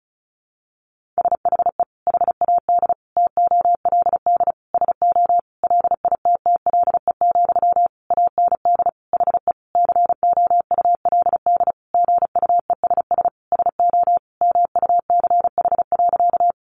Morse code, heard as a keyed tone, about 34 wpm, 700 Hz. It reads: SHE HAD TOLD SO LITTLE, AND HE COULD GUESS SO MUCH.
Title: Sentences from Top 1500 Words 35wpm